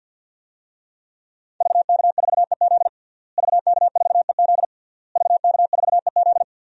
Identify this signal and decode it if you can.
Morse code, heard as a keyed tone, about 52 wpm, 700 Hz. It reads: VK4EZ VK4EZ VK4EZ
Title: DXCC-100 Call Signs - 5 Characters - Repeated 3x - 50wpm